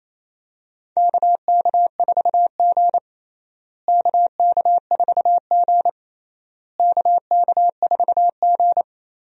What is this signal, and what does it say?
Morse code, heard as a keyed tone, about 28 wpm, 700 Hz. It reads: KK4G KK4G KK4G
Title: USA Call Signs - 4 Characters - Repeated 3x - 28wpm